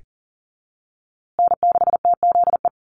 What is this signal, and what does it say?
Morse code, heard as a keyed tone, about 40 wpm, 700 Hz. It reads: N6TZE
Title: USA Call Signs Repeated 2x 40wpm